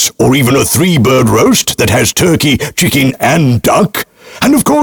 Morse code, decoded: none